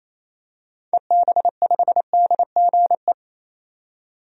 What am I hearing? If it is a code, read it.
Morse code, heard as a keyed tone, about 28 wpm, 700 Hz. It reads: EB5DGE
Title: DXCC-100 Call Signs - 6 Characters - 28wpm